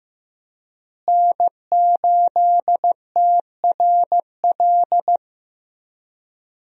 Morse code, heard as a keyed tone, about 15 wpm, 700 Hz. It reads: N8TRL
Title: USA Call Signs - 5 Characters - Repeated 2x - 15wpm